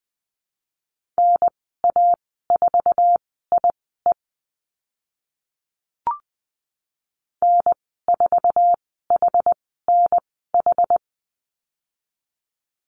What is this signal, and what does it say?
Morse code, heard as a keyed tone, about 20 wpm, spaced as for 14 wpm, 700 Hz. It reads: NA4IE N4HNH
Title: 076 Lesson 13g - Letter L - Call Signs 20wpm